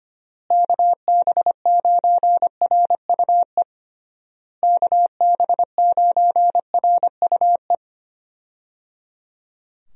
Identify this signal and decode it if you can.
Morse code, heard as a keyed tone, about 25 wpm, 700 Hz. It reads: KB9RUE KB9RUE
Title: USA Call Signs - 6 Characters - Repeated 3x - 25wpm